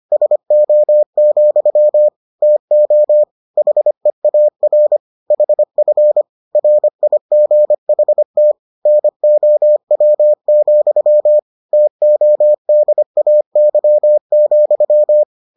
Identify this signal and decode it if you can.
Morse code, heard as a keyed tone, about 25 wpm, 600 Hz. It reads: SO, TO HEAR HF RIGHT NOW, TODAY,